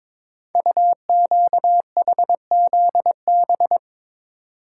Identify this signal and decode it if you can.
Morse code, heard as a keyed tone, about 22 wpm, 700 Hz. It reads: UQHZB
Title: Code Groups - 5 Letters - 22wpm